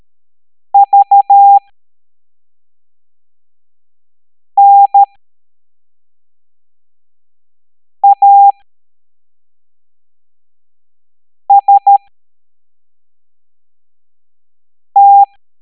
Morse code, heard as a keyed tone, about 13 wpm, spaced as for 3 wpm, 800 Hz. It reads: VNAST
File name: hamdata / L09-EX04